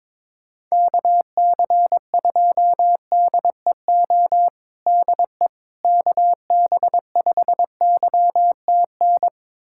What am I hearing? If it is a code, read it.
Morse code, heard as a keyed tone, about 22 wpm, 700 Hz. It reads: KC2DEO DE KB5YTN
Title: USA Callsign DE Callsign 22wpm